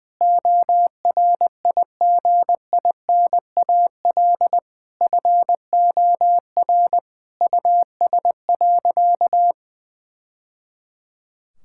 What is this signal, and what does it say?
Morse code, heard as a keyed tone, about 20 wpm, 700 Hz. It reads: ORIGINAL FOR US.
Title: Sentences from Top 1000 Words - Review Part 6 - 20wpm